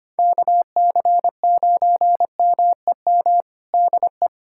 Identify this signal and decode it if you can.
Morse code, heard as a keyed tone, about 25 wpm, 700 Hz. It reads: KC9MEM DE
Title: USA Callsign DE Callsign - 6 Characters - 25wpm